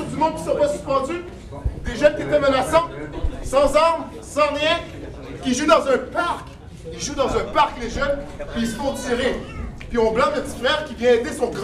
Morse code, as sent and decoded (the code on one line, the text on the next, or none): none